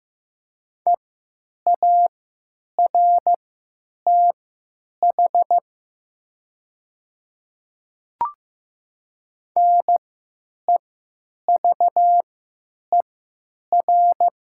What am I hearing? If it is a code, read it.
Morse code, heard as a keyed tone, about 15 wpm, spaced as for 9 wpm, 700 Hz. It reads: EARTH NEVER
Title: 3x Character Spacing - Top 300 Words - 15wpm